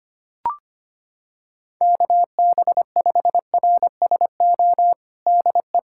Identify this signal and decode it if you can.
Morse code, heard as a keyed tone, about 25 wpm, 700 Hz. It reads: KB5RSO DE
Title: USA Callsign DE Callsign - 6 Characters - 25wpm